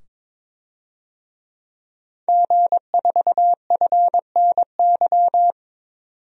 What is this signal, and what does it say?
Morse code, heard as a keyed tone, about 22 wpm, 700 Hz. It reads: G4FNY
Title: DXCC-100 Call Signs - 5 Characters - 22wpm